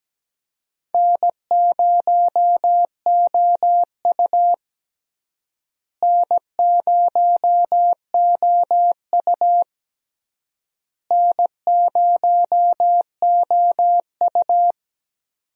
Morse code, heard as a keyed tone, about 17 wpm, 700 Hz. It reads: N0OU N0OU N0OU
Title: USA Call Signs - 4 Characters - Repeated 3x - 17wpm